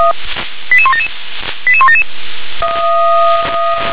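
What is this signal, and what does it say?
Five-tone selective call (ZVEI), heard as quick runs of five tones, tones of about 70 ms. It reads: 88177 88177
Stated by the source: ZVEI five-tone selective call received on an SDR.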